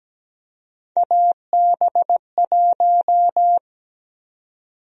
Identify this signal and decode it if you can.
Morse code, heard as a keyed tone, about 17 wpm, 700 Hz. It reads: AB1
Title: USA Call Sign Prefixes - 17wpm